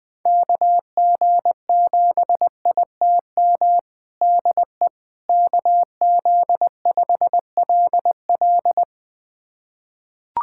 Morse code, heard as a keyed tone, about 21 wpm, 700 Hz. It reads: KG7ITM DE KZ5LL
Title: USA Callsign DE Callsign 20wpm